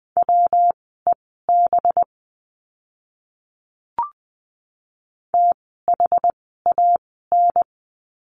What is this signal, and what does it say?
Morse code, heard as a keyed tone, about 20 wpm, spaced as for 14 wpm, 700 Hz. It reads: WEB THAN
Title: 185 Lesson 30f - Letter b - Words 20wpm